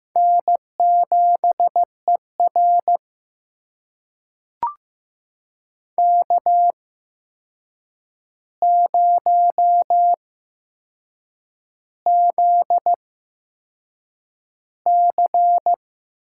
Morse code, heard as a keyed tone, about 15 wpm, spaced as for 6 wpm, 700 Hz. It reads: N7ER K0ZC
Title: USA Call Signs - 4 Characters - 8x Character Spacing - 15wpm